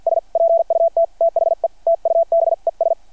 Morse code, heard as a keyed tone, about 51 wpm, spaced as for 45 wpm, 660 Hz. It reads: S PUT THE TUBES
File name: archived / CRSnet